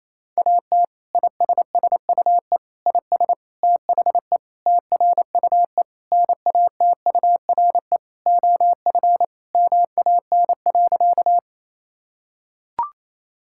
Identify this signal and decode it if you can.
Morse code, heard as a keyed tone, about 27 wpm, 700 Hz. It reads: AT ISSUE IS THE TRUE NATURE OF MAN.
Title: Sentences from Top 1000 Words - Review Part 6 - 28wpm